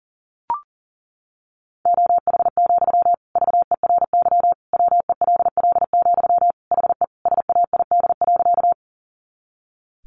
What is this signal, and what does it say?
Morse code, heard as a keyed tone, about 40 wpm, 700 Hz. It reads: OH, VERY WELL, HE SAID.
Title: Sentences from Top 500 Words 40wpm